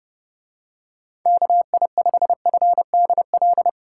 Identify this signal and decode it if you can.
Morse code, heard as a keyed tone, about 29 wpm, 700 Hz. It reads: KI5FDL